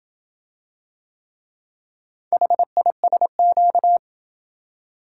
Morse code, heard as a keyed tone, about 26 wpm, 700 Hz. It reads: HISQ